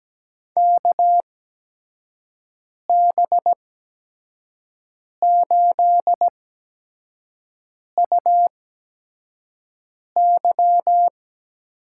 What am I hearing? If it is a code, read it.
Morse code, heard as a keyed tone, about 17 wpm, spaced as for 5 wpm, 700 Hz. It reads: KB8UY
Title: USA Call Signs - 6 Characters - 8x Character Spacing - 17wpm